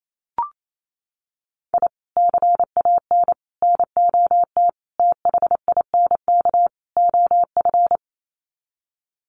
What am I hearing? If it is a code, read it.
Morse code, heard as a keyed tone, about 28 wpm, 700 Hz. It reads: I CAN NOT THINK OF
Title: Sets of 5 Words 28wpm